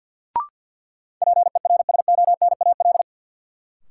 Morse code, heard as a keyed tone, about 51 wpm, 700 Hz. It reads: PERSONAL
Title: Sentences from Top 700 Words - New Word Drill - 50wpm